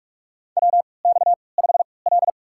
Morse code, heard as a keyed tone, about 43 wpm, spaced as for 25 wpm, 700 Hz. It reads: WX5L
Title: USA Call Signs - 4 Characters - 3x Character Spacing - 45wpm